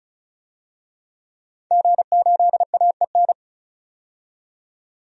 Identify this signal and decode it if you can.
Morse code, heard as a keyed tone, about 36 wpm, 700 Hz. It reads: G8AEN
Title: DXCC-100 Call Signs - 5 Characters - 35wpm